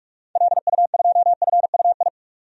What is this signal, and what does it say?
Morse code, heard as a keyed tone, about 46 wpm, 700 Hz. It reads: LU2FUI
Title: DXCC-100 Call Signs - 6 Characters - 45wpm